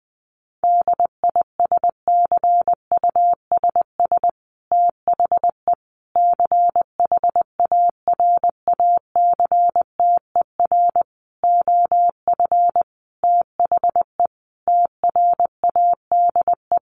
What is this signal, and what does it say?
Morse code, heard as a keyed tone, about 20 wpm, 700 Hz. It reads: DISCUSS THE CHARACTER OF THE TRADE
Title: Sentences from Top 1500 Words 20wpm